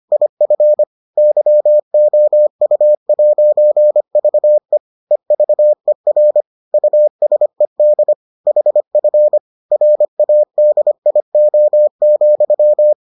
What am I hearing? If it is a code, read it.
Morse code, heard as a keyed tone, about 25 wpm, 600 Hz. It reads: IF YOU'VE EVER USED HF RADIO,